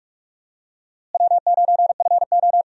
Morse code, heard as a keyed tone, about 41 wpm, 700 Hz. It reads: W9FO
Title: USA Call Signs - 4 Characters - Repeated 3x - 45wpm